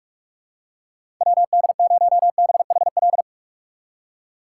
Morse code, heard as a keyed tone, about 46 wpm, 700 Hz. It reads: WD0BHL
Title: USA Call Signs - 6x Character Spacing - 45wpm